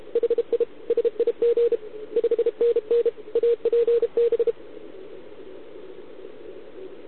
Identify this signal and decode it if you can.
Morse code, heard as a keyed tone, about 33 wpm, 470 Hz. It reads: HI SIG 5NN APB